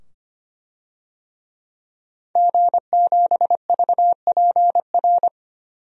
Morse code, heard as a keyed tone, about 25 wpm, 700 Hz. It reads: G7VPR
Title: DXCC-100 Call Signs - 5 Characters - 25wpm